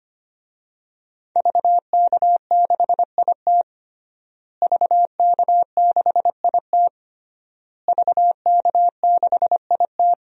Morse code, heard as a keyed tone, about 25 wpm, 700 Hz. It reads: VK6IT VK6IT VK6IT
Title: DXCC-100 Call Signs - 5 Characters - Repeated 3x - 25wpm